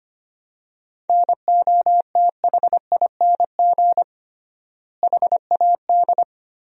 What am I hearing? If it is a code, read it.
Morse code, heard as a keyed tone, about 25 wpm, 700 Hz. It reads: NOTHING HAD
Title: 3x Word-Spacing - Sentences from Top 500 Words - 25wpm